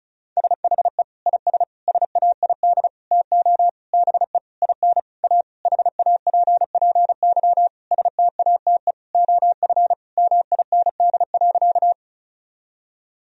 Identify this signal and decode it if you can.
Morse code, heard as a keyed tone, about 35 wpm, 700 Hz. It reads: SHE IS SAID TO BE IN A HAPPY STATE OF MIND.
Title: Sentences from Top 1000 Words - Review Part 2 - 35wpm